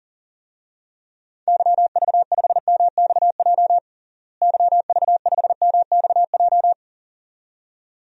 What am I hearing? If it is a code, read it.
Morse code, heard as a keyed tone, about 40 wpm, 700 Hz. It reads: YV5MXJ YV5MXJ
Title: DXCC-100 Call Signs - 6 Characters - Repeated 2x - 40wpm